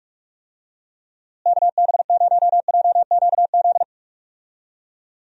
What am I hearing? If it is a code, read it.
Morse code, heard as a keyed tone, about 46 wpm, 700 Hz. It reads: KB0JQZ
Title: USA Call Signs - 6 Characters - Repeated 3x - 45wpm